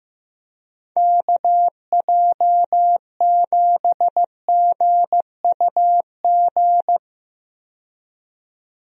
Morse code, heard as a keyed tone, about 15 wpm, 700 Hz. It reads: KJ7GUG
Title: USA Call Signs - 6 Characters - 15wpm